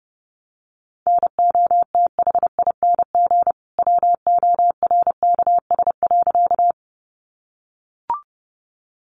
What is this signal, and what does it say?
Morse code, heard as a keyed tone, about 30 wpm, 700 Hz. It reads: NOTHING WORKS.
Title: Additional Easy Phrases 30wpm